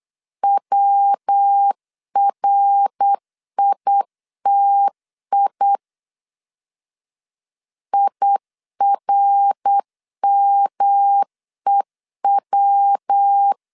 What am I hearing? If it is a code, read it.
Morse code, heard as a keyed tone, about 8 wpm, 800 Hz. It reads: WRITI IRMEW
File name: CW / DET / 1 3 1500C S C 8wpm 4214